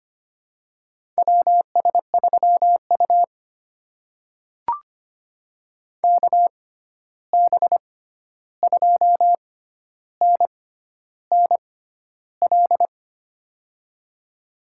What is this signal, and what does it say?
Morse code, heard as a keyed tone, about 25 wpm, spaced as for 11 wpm, 700 Hz. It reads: WS3U KB2NNL
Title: USA Call Signs - 6x Character Spacing - 25wpm